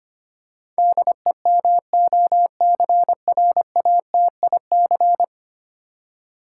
Morse code, heard as a keyed tone, about 25 wpm, 700 Hz. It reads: DEMOCRATIC
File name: Ten Letter Words - 25wpm